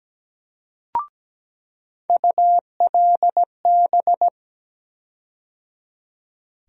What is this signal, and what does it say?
Morse code, heard as a keyed tone, about 17 wpm, 700 Hz. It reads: ULB
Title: USA Call Sign Postfixes - 3 Letters 17wpm